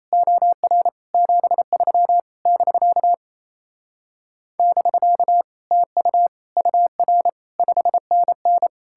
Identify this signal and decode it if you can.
Morse code, heard as a keyed tone, about 37 wpm, 700 Hz. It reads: OR 73 <BK> <BK> TU UR 5NN